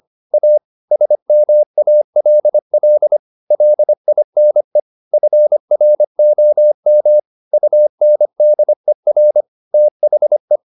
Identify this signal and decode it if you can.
Morse code, heard as a keyed tone, about 25 wpm, 600 Hz. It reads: A SMALL LINE FROM UNDER THE
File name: the old man and the sea part 1 25wpm